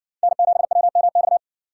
Morse code, heard as a keyed tone, about 59 wpm, 700 Hz. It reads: N7WMX